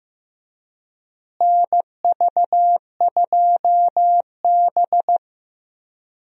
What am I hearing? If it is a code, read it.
Morse code, heard as a keyed tone, about 15 wpm, 700 Hz. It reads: NV2B
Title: USA Call Signs - 4 Characters - Repeated 2x - 15wpm